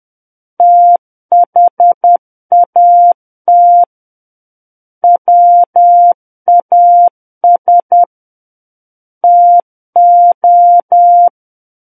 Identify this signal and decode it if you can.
Morse code, heard as a keyed tone, about 10 wpm, 700 Hz. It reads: THAT WAS TO
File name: Book One / 01-Chapter 1 - 10 WPM